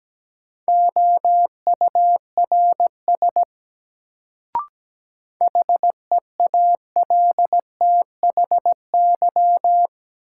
Morse code, heard as a keyed tone, about 17 wpm, 700 Hz. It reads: OURS HEALTHY
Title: Sentences from Top 1000 Words - Review Part 5 - New Word Drill - 17wpm